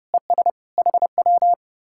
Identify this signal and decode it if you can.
Morse code, heard as a keyed tone, about 40 wpm, 700 Hz. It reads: ES HW